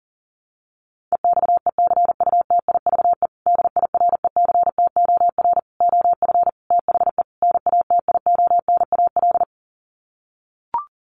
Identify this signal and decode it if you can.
Morse code, heard as a keyed tone, about 40 wpm, 700 Hz. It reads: EXECUTIVE DIRECTOR OF THE NATIONAL
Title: Sets of 5 Words Encore2 40wpm